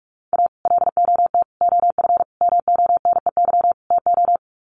Morse code, heard as a keyed tone, about 44 wpm, 700 Hz. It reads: A LOT OF MONEY TO
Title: Sets of 5 Words Encore2 45wpm